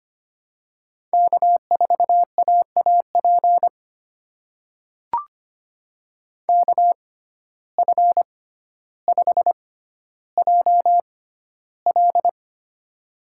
Morse code, heard as a keyed tone, about 25 wpm, spaced as for 13 wpm, 700 Hz. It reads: K4AAP KF5JL
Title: USA Call Signs - 5 Characters - 6x Character Spacing - 25wpm